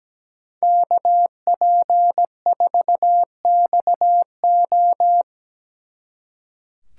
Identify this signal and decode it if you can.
Morse code, heard as a keyed tone, about 21 wpm, 700 Hz. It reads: KP4XO
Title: Speed Racing - USA 2x2 Call Signs 17wpm